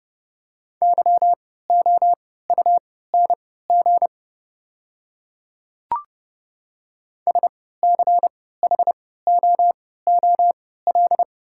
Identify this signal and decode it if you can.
Morse code, heard as a keyed tone, about 30 wpm, spaced as for 17 wpm, 700 Hz. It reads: YOUNG SCHOOL
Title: 3x Character Spacing - Top 500 Words - 30wpm